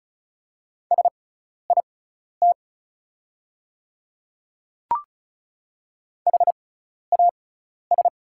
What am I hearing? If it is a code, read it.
Morse code, heard as a keyed tone, about 34 wpm, spaced as for 12 wpm, 690 Hz. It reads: SIT HAS
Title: 6x Character Spacing - Top 500 Words - 35wpm